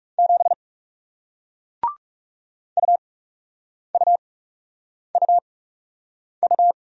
Morse code, heard as a keyed tone, about 40 wpm, spaced as for 8 wpm, 700 Hz. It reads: 7 UUUU